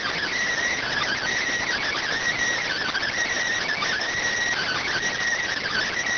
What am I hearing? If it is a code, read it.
Emergency Alert System (SAME) tones (header bursts), heard as harsh buzzing bursts. It is heard from 0.7 to 5.0 s.